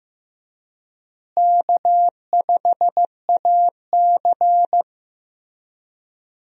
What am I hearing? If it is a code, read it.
Morse code, heard as a keyed tone, about 16 wpm, 700 Hz. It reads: K5AC
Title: USA Call Signs - 4 Characters - Repeated 2x - 15wpm